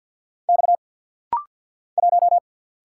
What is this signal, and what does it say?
Morse code, heard as a keyed tone, about 49 wpm, 690 Hz. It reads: X 1
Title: Single Letter-Number - Rapid-Fire 50wpm